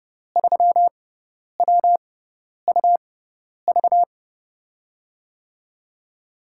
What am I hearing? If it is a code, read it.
Morse code, heard as a keyed tone, about 30 wpm, spaced as for 10 wpm, 700 Hz. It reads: 3WUV
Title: USA Call Signs - 5 Characters - 6x Character Spacing - 30wpm